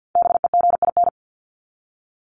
Morse code, heard as a keyed tone, about 48 wpm, 700 Hz. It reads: BEGIN
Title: Top 300 Words 50wpm